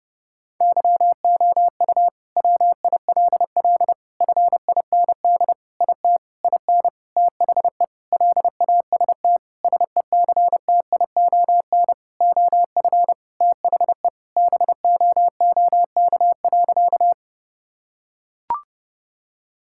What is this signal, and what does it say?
Morse code, heard as a keyed tone, about 30 wpm, 700 Hz. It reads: YOU WILL FIND IT IN THE LAST SECTION OF THE BOOK.
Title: Sentences from Top 1000 Words - Review Part 5 - 30wpm